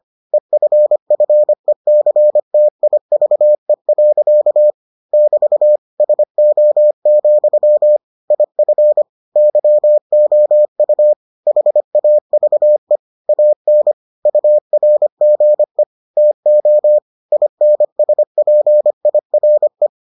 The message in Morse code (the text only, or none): EFFECTIVE. <BT> SO, IF YOU HAVE AN URGE TO INSPIRE